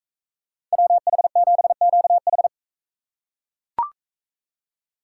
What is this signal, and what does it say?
Morse code, heard as a keyed tone, about 40 wpm, 700 Hz. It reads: WH7QH